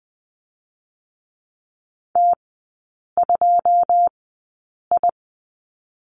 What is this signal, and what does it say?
Morse code, heard as a keyed tone, about 20 wpm, 700 Hz. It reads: T 2 I
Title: 107 Lesson 18e - Letter m - Three Characters 20wpm